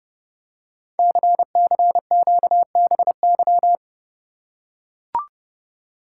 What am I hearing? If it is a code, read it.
Morse code, heard as a keyed tone, about 30 wpm, 700 Hz. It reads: CCQBY
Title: Code Groups - 5 Letters - 30wpm